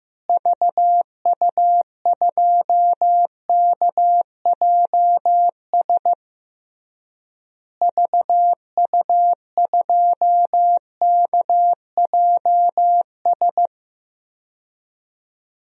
Morse code, heard as a keyed tone, about 15 wpm, 700 Hz. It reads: VU2KJS VU2KJS
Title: DXCC-100 Call Signs - 6 Characters - Repeated 3x - 15wpm